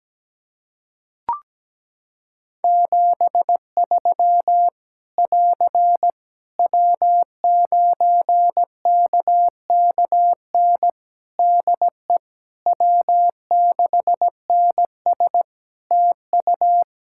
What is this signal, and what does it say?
Morse code, heard as a keyed tone, about 17 wpm, 700 Hz. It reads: 73 <AR> W9KKN DE W6NS TU
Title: Practice QSO - Exchange 4 - Ending - 17wpm